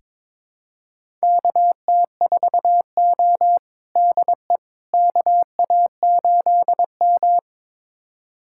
Morse code, heard as a keyed tone, about 22 wpm, 700 Hz. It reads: KT4O DE KA8M
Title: USA Callsign DE Callsign - 4 Characters - 22wpm